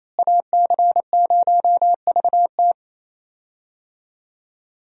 Morse code, heard as a keyed tone, about 28 wpm, 700 Hz. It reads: AC0VT
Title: USA Call Signs - 5 Characters - Repeated 2x - 28wpm